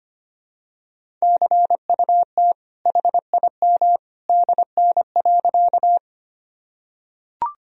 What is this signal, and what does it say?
Morse code, heard as a keyed tone, about 25 wpm, 700 Hz. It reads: CUT HIM DN.
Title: Sentences from Top 300 Words and Abbreviations 25wpm